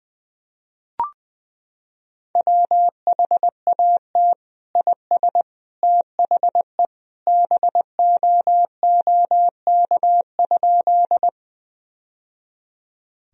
Morse code, heard as a keyed tone, about 19 wpm, 700 Hz. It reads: WHAT IS THE BOOK?
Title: Sentences from Top 1000 Words - Review Part 1 - 20wpm